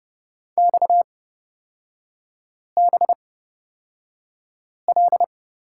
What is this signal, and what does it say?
Morse code, heard as a keyed tone, about 30 wpm, spaced as for 5 wpm, 700 Hz. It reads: XBL
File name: Lesson 37e - Letter x - Three Characters 30wpm